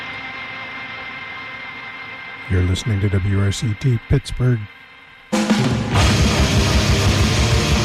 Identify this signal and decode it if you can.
Dial-up modem handshake, with speech.
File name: wrct-log hi- 00